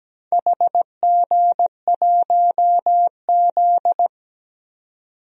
Morse code, heard as a keyed tone, about 17 wpm, 700 Hz. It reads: HG1Z
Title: DXCC-100 Call Signs - 4 Characters - 17wpm